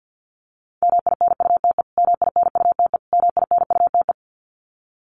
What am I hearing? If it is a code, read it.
Morse code, heard as a keyed tone, about 48 wpm, 700 Hz. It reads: MINUTE MINUTE MINUTE
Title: Top 500 Words Repeated 50wpm